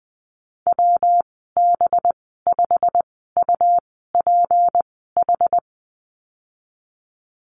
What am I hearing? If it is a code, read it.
Morse code, heard as a keyed tone, about 20 wpm, spaced as for 14 wpm, 700 Hz. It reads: WB5UPH
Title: 193 Lesson 31g - Letter v - Call Signs 20wpm